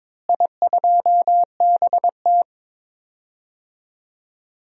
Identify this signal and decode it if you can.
Morse code, heard as a keyed tone, about 22 wpm, 700 Hz. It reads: I2BT
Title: DXCC-100 Call Signs - 4 Characters - Repeated 3x - 22wpm